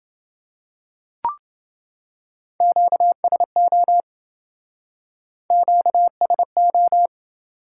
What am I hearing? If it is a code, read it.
Morse code, heard as a keyed tone, about 28 wpm, 700 Hz. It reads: QSO QSO